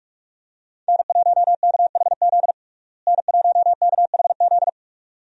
Morse code, heard as a keyed tone, about 46 wpm, 700 Hz. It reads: N1KHZ N1KHZ
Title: USA Call Signs - 5 Characters - Repeated 2x - 45wpm